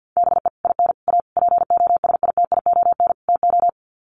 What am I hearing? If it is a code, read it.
Morse code, heard as a keyed tone, about 50 wpm, 700 Hz. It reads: BE IN A POSITION TO
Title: Sets of 5 Words Encore2 50wpm